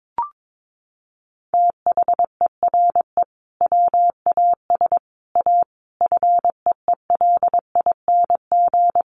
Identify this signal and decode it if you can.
Morse code, heard as a keyed tone, about 22 wpm, 700 Hz. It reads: THERE WAS A FEELING
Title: Sentences from Top 1200 Words 22wpm